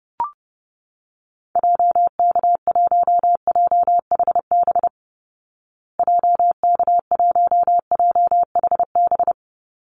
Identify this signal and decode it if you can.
Morse code, heard as a keyed tone, about 30 wpm, 700 Hz. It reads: JK1JHB JK1JHB
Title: Call Signs Repeated 2x - 30wpm